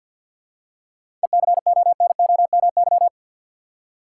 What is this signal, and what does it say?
Morse code, heard as a keyed tone, about 52 wpm, 700 Hz. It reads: ECONOMY
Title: Seven Letter Words - 50wpm